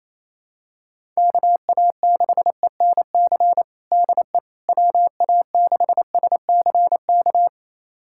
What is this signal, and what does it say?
Morse code, heard as a keyed tone, about 28 wpm, 700 Hz. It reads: KA6ENC DE WA6SCK
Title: USA Callsign DE Callsign - 6 Characters - 28wpm